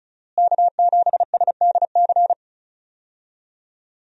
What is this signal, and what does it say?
Morse code, heard as a keyed tone, about 36 wpm, 700 Hz. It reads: K7SDC
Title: USA Call Signs - 5 Characters - Repeated 3x - 35wpm